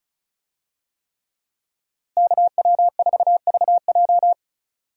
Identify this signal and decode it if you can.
Morse code, heard as a keyed tone, about 35 wpm, 700 Hz. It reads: KW4VJ
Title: USA Call Signs - 5 Characters - Repeated 3x - 35wpm